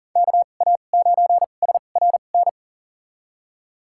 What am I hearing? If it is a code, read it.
Morse code, heard as a keyed tone, about 40 wpm, spaced as for 29 wpm, 700 Hz. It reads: KA9SRN
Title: USA Call Signs - 6 Characters - 2x Character Spacing - 40wpm